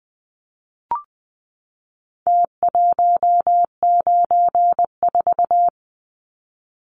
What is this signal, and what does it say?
Morse code, heard as a keyed tone, about 20 wpm, 700 Hz. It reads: T194
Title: Numbers - Contest Style - 4-Digit 20wpm